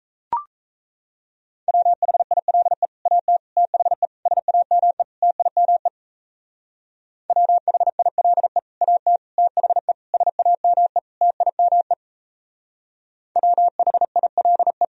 Morse code, heard as a keyed tone, about 34 wpm, 700 Hz. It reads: WHILE AT THE SAME TIME WHILE AT THE SAME TIME WHILE
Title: Speed-Racing - Sets of 5 Words 28wpm